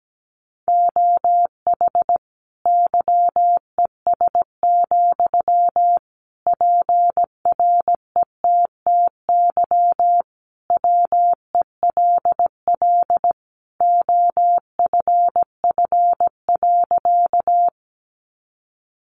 Morse code, heard as a keyed tone, about 17 wpm, 700 Hz. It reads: OH YES, PRETTY WELL OFF.
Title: Sentences from Top 1000 Words 17wpm